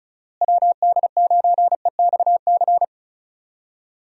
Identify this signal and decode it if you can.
Morse code, heard as a keyed tone, about 36 wpm, 700 Hz. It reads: WD9EXC
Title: USA Call Signs - 6 Characters - Repeated 3x - 35wpm